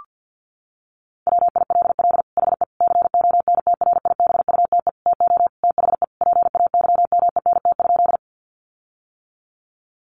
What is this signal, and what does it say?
Morse code, heard as a keyed tone, about 50 wpm, 700 Hz. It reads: WILL HE CONTRIBUTE TO THE PAYMENT?
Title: Sentences from Top 1400 Words 50wpm